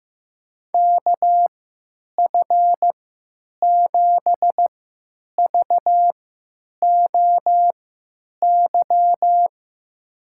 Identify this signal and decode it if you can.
Morse code, heard as a keyed tone, about 15 wpm, spaced as for 9 wpm, 700 Hz. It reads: KF7VOY
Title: USA Call Signs - 6 Characters - 3x Character Spacing - 15wpm